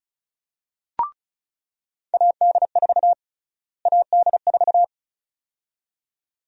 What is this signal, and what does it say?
Morse code, heard as a keyed tone, about 36 wpm, 700 Hz. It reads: AD4 AD4
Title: USA Call Sign Prefixes - Repeated 2x - 35wpm